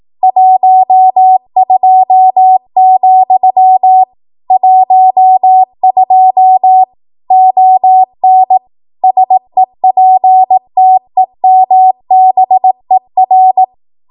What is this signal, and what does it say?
Morse code, heard as a keyed tone, about 18 wpm, 750 Hz. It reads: 12, 12 ON SEPTEMBER
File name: Archive / 18 WPM / 18WPM